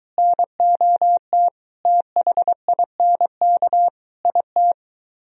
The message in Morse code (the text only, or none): NOT THINK IT